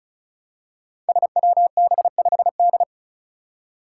Morse code, heard as a keyed tone, about 36 wpm, 700 Hz. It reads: SWB5D
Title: Five Challenging Letter-Number Combinations 35wpm